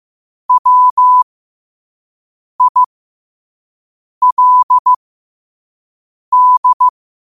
Morse code, heard as a keyed tone, about 15 wpm, spaced as for 5 wpm, 1000 Hz. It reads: WILD
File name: qotdfiles / 5wpm